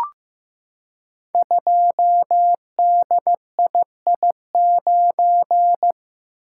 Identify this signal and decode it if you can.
Morse code, heard as a keyed tone, about 15 wpm, 700 Hz. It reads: 2DII9
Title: Code Groups - 5 Letter:Numbers - 15wpm